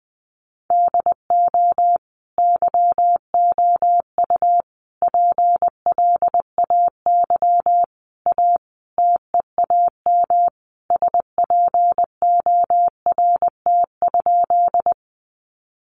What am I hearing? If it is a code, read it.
Morse code, heard as a keyed tone, about 20 wpm, 700 Hz. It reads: DO YOU PLAY A TEAM SPORT?
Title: Sentences from Top 1500 Words 20wpm